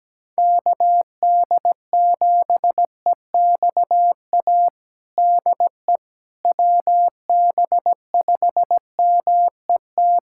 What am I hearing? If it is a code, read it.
Morse code, heard as a keyed tone, about 17 wpm, 700 Hz. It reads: KD7EXA DE WB5MET
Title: USA Callsign DE Callsign - 6 Characters - 17wpm